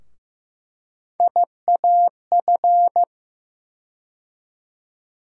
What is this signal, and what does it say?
Morse code, heard as a keyed tone, about 15 wpm, 700 Hz. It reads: IAF
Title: USA Call Sign Postfixes - 3 Letters 15wpm